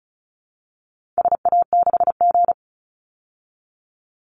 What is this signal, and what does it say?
Morse code, heard as a keyed tone, about 34 wpm, 700 Hz. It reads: SA6G
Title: Call Signs Repeated 2x - 35wpm